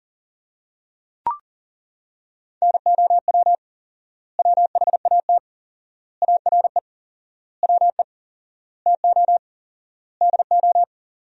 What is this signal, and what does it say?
Morse code, heard as a keyed tone, about 40 wpm, 700 Hz. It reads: NOW WHAT ARE WE TO DO
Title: 4x Word-Spacing - Sentences from Top 100 Words - 40wpm